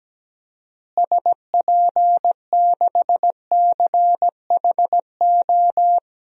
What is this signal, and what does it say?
Morse code, heard as a keyed tone, about 17 wpm, 700 Hz. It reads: SP6CHO
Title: DXCC-100 Call Signs - 6 Characters - Repeated 3x - 17wpm